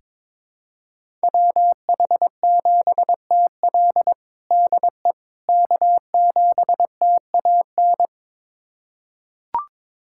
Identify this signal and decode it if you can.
Morse code, heard as a keyed tone, about 21 wpm, 700 Hz. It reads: WH7TL DE K7TAN
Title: USA Callsign DE Callsign - 5 Characters - 22wpm